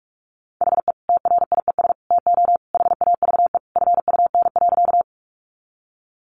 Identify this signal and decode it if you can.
Morse code, heard as a keyed tone, about 44 wpm, 700 Hz. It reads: HE TRIES TO HAVE FUN.
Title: Easy Phrases 45wpm